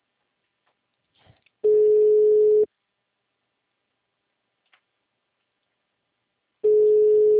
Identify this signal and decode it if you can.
Ringback tone (Europe, 425 Hz).
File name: out- -2025- - - .1167